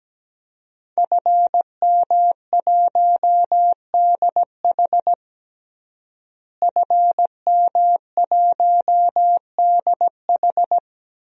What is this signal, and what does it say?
Morse code, heard as a keyed tone, about 17 wpm, 700 Hz. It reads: FM1DH FM1DH
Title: DXCC-100 Call Signs - 5 Characters - Repeated 2x - 17wpm